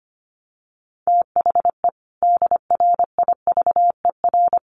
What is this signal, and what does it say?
Morse code, heard as a keyed tone, about 25 wpm, 700 Hz. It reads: THE DRIVER
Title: Sentences from Top 1000 Words 25wpm